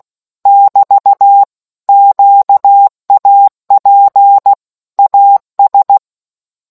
Morse code, heard as a keyed tone, about 16 wpm, 800 Hz. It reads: <BT> QAP AS